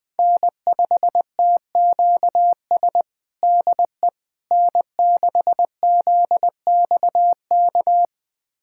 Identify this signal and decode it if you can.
Morse code, heard as a keyed tone, about 20 wpm, 700 Hz. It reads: N5TQS DE N6ZXK
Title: USA Callsign DE Callsign - 5 Characters - 20wpm